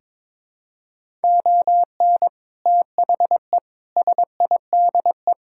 Morse code, heard as a keyed tone, about 22 wpm, 700 Hz. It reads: ON THE SIDE
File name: practice / 22wpm / Sentences with 3-Word Phrases - 22wpm